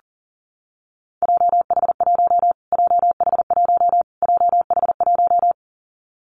Morse code, heard as a keyed tone, about 37 wpm, 700 Hz. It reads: JH1 JH1 JH1
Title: Callsign Prefixes for Top 25 DXCC - Repeated 3x - 40wpm